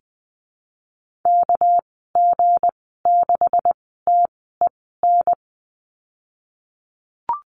Morse code, heard as a keyed tone, about 19 wpm, spaced as for 14 wpm, 700 Hz. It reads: KG6TEN
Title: 200 Lesson 32g - Letter k - Call Signs 20wpm